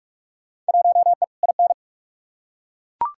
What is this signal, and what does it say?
Morse code, heard as a keyed tone, about 42 wpm, 700 Hz. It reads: 1E IN
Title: Field Day Exchanges - 45wpm